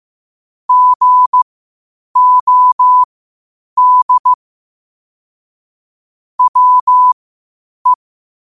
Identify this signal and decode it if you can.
Morse code, heard as a keyed tone, about 15 wpm, spaced as for 8 wpm, 1000 Hz. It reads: GOD WE
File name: qotdfiles / 8wpm